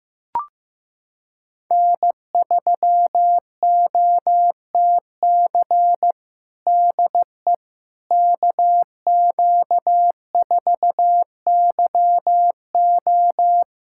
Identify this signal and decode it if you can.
Morse code, heard as a keyed tone, about 15 wpm, 700 Hz. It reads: N3OTC DE KQ4YO
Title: USA Callsign DE Callsign - 5 Characters - 15wpm